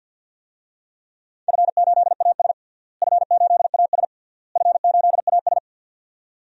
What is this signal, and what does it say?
Morse code, heard as a keyed tone, about 48 wpm, 700 Hz. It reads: F8AS F8AS F8AS
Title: DXCC-100 Call Signs - 4 Characters - Repeated 3x - 50wpm